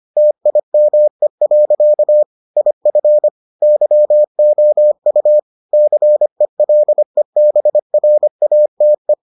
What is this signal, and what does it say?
Morse code, heard as a keyed tone, about 25 wpm, 600 Hz. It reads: TIME. IF YOU CELEBRATE